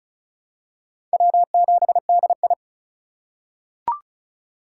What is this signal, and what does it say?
Morse code, heard as a keyed tone, about 36 wpm, 700 Hz. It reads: W7DI
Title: USA Call Signs - 4 Characters - 4x Character Spacing - 35wpm